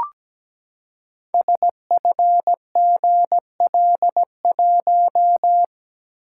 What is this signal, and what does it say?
Morse code, heard as a keyed tone, about 17 wpm, 700 Hz. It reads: SFGL1
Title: Code Groups - 5 Letter:Numbers - 17wpm